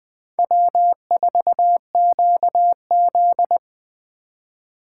Morse code, heard as a keyed tone, about 20 wpm, 700 Hz. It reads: W4QZ
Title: Speed Racing - USA Call Signs 17wpm